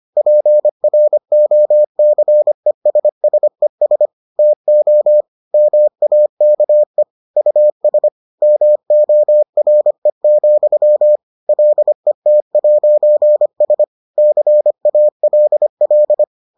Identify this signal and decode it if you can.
Morse code, heard as a keyed tone, about 25 wpm, 600 Hz. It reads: PROCESSES TO MAKE US MORE, LET'S CALL